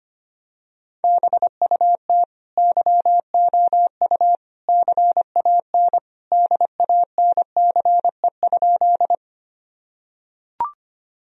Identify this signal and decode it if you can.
Morse code, heard as a keyed tone, about 25 wpm, 700 Hz. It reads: BUT YOU CAN DANCE?
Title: Sentences from Top 1000 Words - Review Part 4 - 25wpm